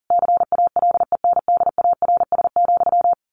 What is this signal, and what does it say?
Morse code, heard as a keyed tone, about 40 wpm, 700 Hz. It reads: CALENDARS,
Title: A Sound Of Thunder 40wpm